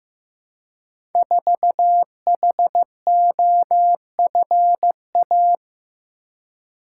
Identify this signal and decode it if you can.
Morse code, heard as a keyed tone, about 15 wpm, 700 Hz. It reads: 4HOFA
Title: Code Groups - 5 Letter:Numbers - 15wpm